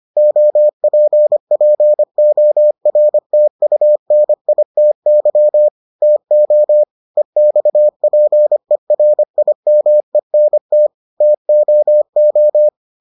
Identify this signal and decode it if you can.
Morse code, heard as a keyed tone, about 25 wpm, 600 Hz. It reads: OPPORTUNITY TO EXPERIMENT TOO